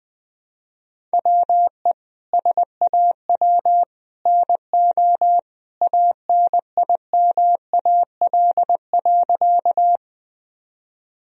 Morse code, heard as a keyed tone, about 20 wpm, 700 Hz. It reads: WE SAW NO ANIMAL.
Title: Sentences from Top 1000 Words - Review Part 1 - 20wpm